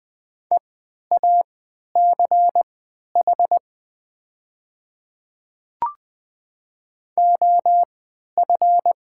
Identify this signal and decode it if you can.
Morse code, heard as a keyed tone, about 20 wpm, spaced as for 11 wpm, 700 Hz. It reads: EACH OF
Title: 3x Character Spacing - Top 100 Words - 20wpm